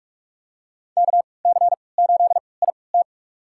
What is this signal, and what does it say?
Morse code, heard as a keyed tone, about 46 wpm, spaced as for 26 wpm, 700 Hz. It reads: KC8IT
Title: USA Call Signs - 5 Characters - 3x Character Spacing - 45wpm